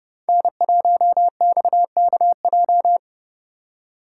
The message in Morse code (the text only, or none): N1XKJ